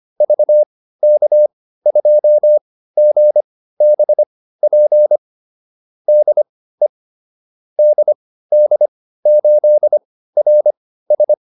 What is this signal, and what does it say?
Morse code, heard as a keyed tone, about 25 wpm, spaced as for 15 wpm, 600 Hz. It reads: VK2GBP DE DD8RS
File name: QSO 405 15